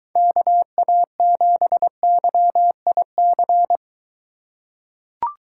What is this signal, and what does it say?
Morse code, heard as a keyed tone, about 23 wpm, 700 Hz. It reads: KA7YIC